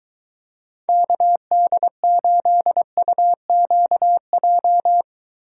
Morse code, heard as a keyed tone, about 22 wpm, 700 Hz. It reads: KD8UQJ